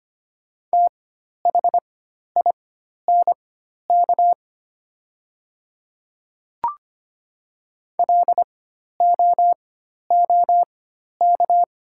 Morse code, heard as a keyed tone, about 25 wpm, spaced as for 12 wpm, 690 Hz. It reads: THINK LOOK
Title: 4x Character Spacing - Top 300 Words - 25wpm